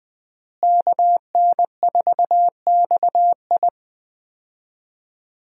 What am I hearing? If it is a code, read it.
Morse code, heard as a keyed tone, about 20 wpm, 700 Hz. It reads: KN4XI
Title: USA Call Signs - 5 Characters - Repeated 3x - 20wpm